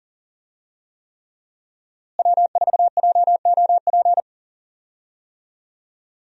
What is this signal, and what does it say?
Morse code, heard as a keyed tone, about 37 wpm, 700 Hz. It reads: W4JOP
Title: USA Call Signs - 5 Characters - Repeated 3x - 40wpm